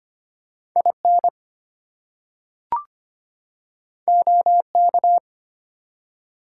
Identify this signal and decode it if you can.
Morse code, heard as a keyed tone, about 25 wpm, 700 Hz. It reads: IN OK
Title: Two Letter Words - 25wpm